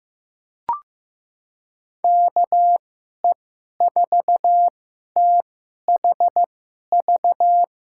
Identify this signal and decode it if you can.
Morse code, heard as a keyed tone, about 15 wpm, spaced as for 11 wpm, 700 Hz. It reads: KE4THV
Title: USA Call Signs - 6 Characters - 2x Character Spacing - 15wpm